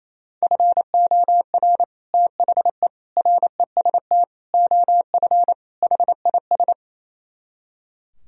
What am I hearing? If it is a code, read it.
Morse code, heard as a keyed tone, about 36 wpm, 700 Hz. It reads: FOR THE REST OF HIS